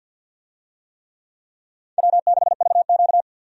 Morse code, heard as a keyed tone, about 52 wpm, 700 Hz. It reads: W6VQ
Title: USA Call Signs - 4 Characters - 50wpm